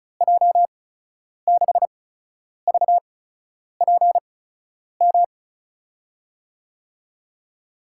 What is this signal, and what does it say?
Morse code, heard as a keyed tone, about 36 wpm, spaced as for 10 wpm, 700 Hz. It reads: J6VPM
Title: USA Call Signs - 6 Characters - 8x Character Spacing - 35wpm